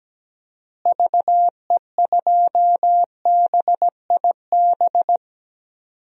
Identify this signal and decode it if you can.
Morse code, heard as a keyed tone, about 17 wpm, 700 Hz. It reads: VE2BIB
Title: DXCC-100 Call Signs - 6 Characters - Repeated 2x - 17wpm